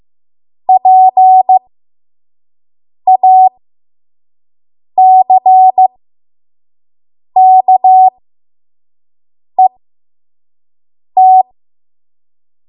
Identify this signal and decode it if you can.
Morse code, heard as a keyed tone, about 15 wpm, spaced as for 5 wpm, 750 Hz. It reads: PACKET
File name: Morse / 05WPM